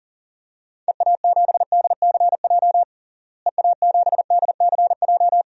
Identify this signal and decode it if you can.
Morse code, heard as a keyed tone, about 37 wpm, 700 Hz. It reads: EA7DCJ EA7DCJ
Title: DXCC-100 Call Signs - 6 Characters - Repeated 2x - 40wpm